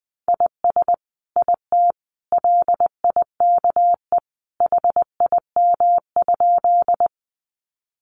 Morse code, heard as a keyed tone, about 20 wpm, 700 Hz. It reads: IS IT LIKE HIM?
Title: Sentences from Top 100 Words 20wpm